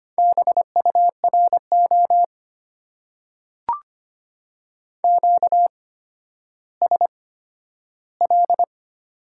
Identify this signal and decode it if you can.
Morse code, heard as a keyed tone, about 25 wpm, spaced as for 12 wpm, 690 Hz. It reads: BURO QSL
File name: Top 90 QSO Elements - 8x Character Spacing - 25wpm